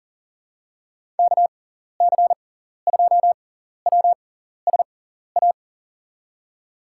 Morse code, heard as a keyed tone, about 40 wpm, spaced as for 14 wpm, 700 Hz. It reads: KC2WSA
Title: USA Call Signs - 6 Characters - 6x Character Spacing - 40wpm